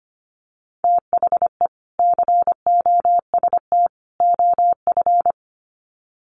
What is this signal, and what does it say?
Morse code, heard as a keyed tone, about 25 wpm, 700 Hz. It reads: THE COST OF
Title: Sets of 3 Words 25wpm